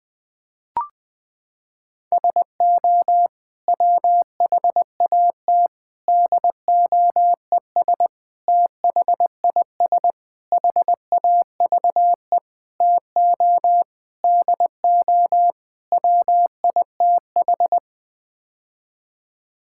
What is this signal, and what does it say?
Morse code, heard as a keyed tone, about 21 wpm, 700 Hz. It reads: SO WHAT DOES THIS HAVE TO DO WITH
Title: Sets of 8 Words - 20wpm